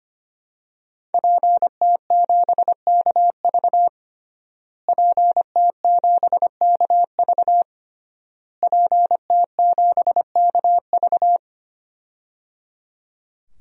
Morse code, heard as a keyed tone, about 25 wpm, 700 Hz. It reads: PT7KV PT7KV PT7KV
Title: DXCC-100 Call Signs - 5 Characters - Repeated 3x - 25wpm